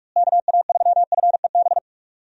Morse code, heard as a keyed tone, about 46 wpm, 700 Hz. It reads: KA3FEB
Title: USA Call Signs - 6 Characters - Repeated 3x - 45wpm